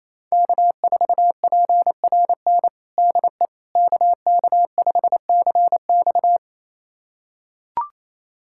Morse code, heard as a keyed tone, about 28 wpm, 700 Hz. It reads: K4PRN DE KK5CX
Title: USA Callsign DE Callsign - 5 Characters - 28wpm